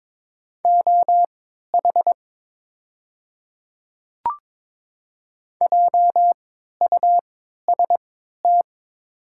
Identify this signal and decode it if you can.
Morse code, heard as a keyed tone, about 22 wpm, spaced as for 12 wpm, 700 Hz. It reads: OH JUST
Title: 3x Character Spacing - Top 500 Words - 22wpm